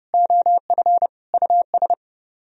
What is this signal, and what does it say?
Morse code, heard as a keyed tone, about 40 wpm, 700 Hz. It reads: OF US